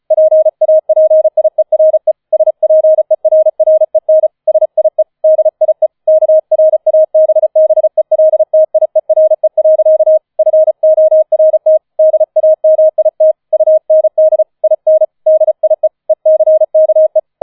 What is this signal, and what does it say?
Morse code, heard as a keyed tone, about 36 wpm, 630 Hz. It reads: PAPIERE SPERREN SIE DIE KRABBELTIERE. FORT DAMIT UND IN DIE ECKE
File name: 005 Fünfter Streich 35wpm